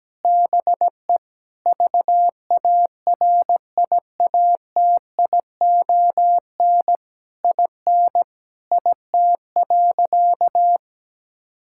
Morse code, heard as a keyed tone, about 17 wpm, 700 Hz. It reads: BE VARIATION IN IT.
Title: Sentences from Top 1600 Words 17wpm